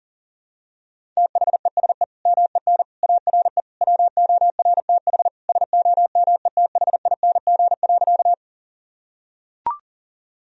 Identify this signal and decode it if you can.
Morse code, heard as a keyed tone, about 40 wpm, 700 Hz. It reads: THESE MEN ARE WORTH SOMETHING.
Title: Sentences from Top 1000 Words - Review Part 1 - 40wpm